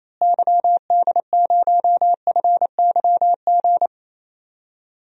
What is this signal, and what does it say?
Morse code, heard as a keyed tone, about 27 wpm, 700 Hz. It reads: YD0FYG
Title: DXCC-100 Call Signs - 6 Characters - Repeated 2x - 28wpm